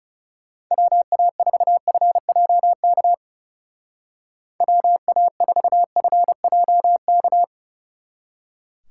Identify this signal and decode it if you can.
Morse code, heard as a keyed tone, about 35 wpm, 700 Hz. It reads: WA4FJK WA4FJK
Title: Speed Racing - USA 2x3 Call Signs 30wpm